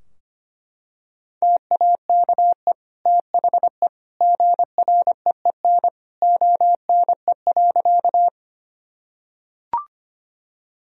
Morse code, heard as a keyed tone, about 25 wpm, 700 Hz. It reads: TAKE THE GREEN ONE.
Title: Sentences from Top 1000 Words - Review Part 2 - 25wpm